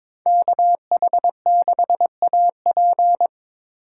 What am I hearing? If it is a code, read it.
Morse code, heard as a keyed tone, about 26 wpm, 700 Hz. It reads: KH6AP
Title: Speed Racing - USA 2x2 Call Signs 22wpm